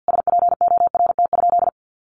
Morse code, heard as a keyed tone, about 50 wpm, 700 Hz. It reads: SPORT?
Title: Sentences from Top 1500 Words 50wpm